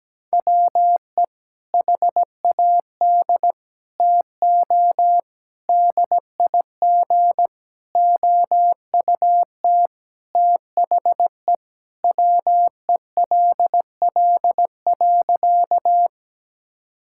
Morse code, heard as a keyed tone, about 17 wpm, 700 Hz. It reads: WE HAD TO DIG OUT THE WELL.
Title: Sentences from Top 1600 Words 17wpm